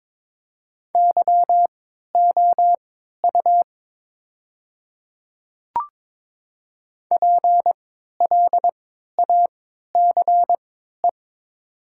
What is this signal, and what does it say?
Morse code, heard as a keyed tone, about 22 wpm, spaced as for 12 wpm, 700 Hz. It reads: YOU PLACE
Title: 3x Character Spacing - Top 500 Words - 22wpm